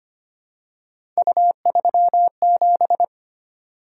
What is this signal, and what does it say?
Morse code, heard as a keyed tone, about 25 wpm, 700 Hz. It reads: U37
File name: Three Letter-Number Combinations - 25wpm